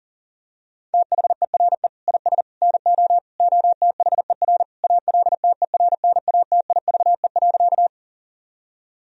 Morse code, heard as a keyed tone, about 40 wpm, 700 Hz. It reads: THERE IS NO OTHER ALTERNATIVE.
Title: Sentences from Top 1000 Words - Review Part 4 - 40wpm